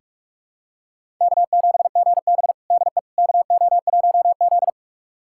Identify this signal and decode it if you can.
Morse code, heard as a keyed tone, about 46 wpm, 700 Hz. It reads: K7GB DE KO1Z
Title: USA Callsign DE Callsign - 4 Characters - 45wpm